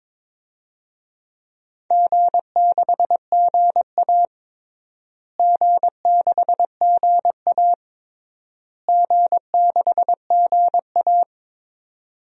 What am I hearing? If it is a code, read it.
Morse code, heard as a keyed tone, about 22 wpm, 700 Hz. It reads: G6GA G6GA G6GA
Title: DXCC-100 Call Signs - 4 Characters - Repeated 3x - 22wpm